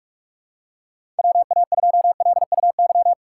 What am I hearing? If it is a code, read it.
Morse code, heard as a keyed tone, about 41 wpm, 700 Hz. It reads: WA2LUY
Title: USA Call Signs - 6 Characters - 4x Character Spacing - 45wpm